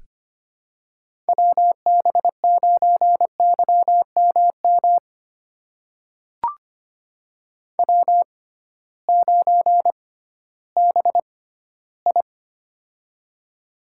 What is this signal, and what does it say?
Morse code, heard as a keyed tone, about 25 wpm, spaced as for 15 wpm, 700 Hz. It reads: WB9YMM W9BI
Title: USA Call Signs - 6x Character Spacing - 25wpm